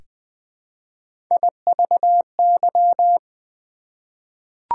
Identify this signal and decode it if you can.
Morse code, heard as a keyed tone, about 20 wpm, 700 Hz. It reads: IVY
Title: Difficult Words with Hints - 20wpm